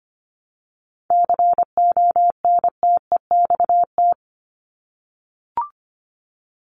Morse code, heard as a keyed tone, about 25 wpm, 700 Hz. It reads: CONTEXT
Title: Top 751-1000 Words 25wpm